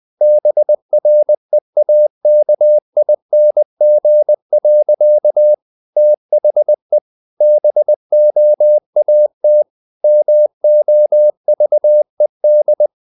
Morse code, heard as a keyed tone, about 20 wpm, 600 Hz. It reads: BREAKING. THE BOAT MOVED
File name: the old man and the sea part 2 20wpm